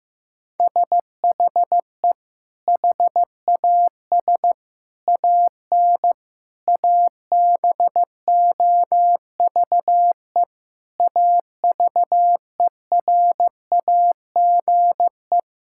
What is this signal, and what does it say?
Morse code, heard as a keyed tone, about 16 wpm, 700 Hz. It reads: SHE HAS AN ABOVE AVERAGE
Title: Sentences from Top 1000 Words - Review Part 6 - 15wpm